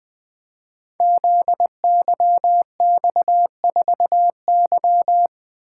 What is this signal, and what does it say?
Morse code, heard as a keyed tone, about 21 wpm, 700 Hz. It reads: ZYX4Y
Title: Code Groups - 5 Letter:Numbers - 20wpm